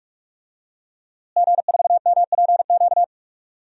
Morse code, heard as a keyed tone, about 46 wpm, 700 Hz. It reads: G4MPQ
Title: DXCC-100 Call Signs - 5 Characters - Repeated 3x - 45wpm